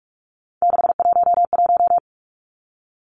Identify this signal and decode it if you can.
Morse code, heard as a keyed tone, about 44 wpm, 700 Hz. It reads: 611
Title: Numbers - 3-Digit 45wpm